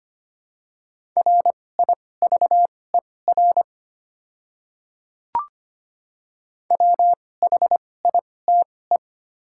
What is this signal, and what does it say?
Morse code, heard as a keyed tone, about 25 wpm, spaced as for 18 wpm, 690 Hz. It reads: RIVER WHITE
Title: 2x Character Spacing - Top 300 Words - 25wpm